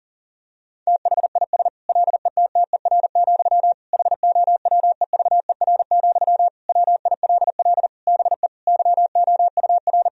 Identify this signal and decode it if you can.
Morse code, heard as a keyed tone, about 37 wpm, 700 Hz. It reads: THIS LETTER, HOWEVER, WILL BE YOUR
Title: Sentences from Top 1000 Words - Review Part 4 - 40wpm